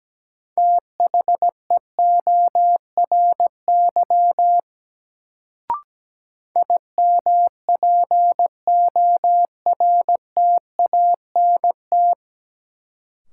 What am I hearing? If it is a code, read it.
Morse code, heard as a keyed tone, about 17 wpm, 700 Hz. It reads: THEORY IMPORTANT
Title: Sentences from Top 600 Words - New Word Drill - 17wpm